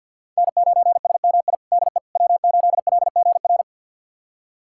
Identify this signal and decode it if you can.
Morse code, heard as a keyed tone, about 49 wpm, 700 Hz. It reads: N9SMI DE W8LGR
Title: USA Callsign DE Callsign - 5 Characters - 50wpm